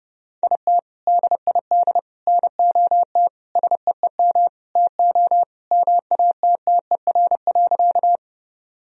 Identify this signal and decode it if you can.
Morse code, heard as a keyed tone, about 30 wpm, 700 Hz. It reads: IT DID NOT SEEM TO MATTER.
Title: Sentences from Top 1000 Words - Review Part 1 - 30wpm